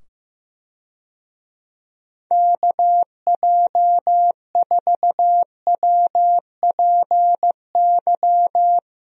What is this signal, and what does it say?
Morse code, heard as a keyed tone, about 15 wpm, 700 Hz. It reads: KJ4WPY
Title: USA Call Signs - 6 Characters - 15wpm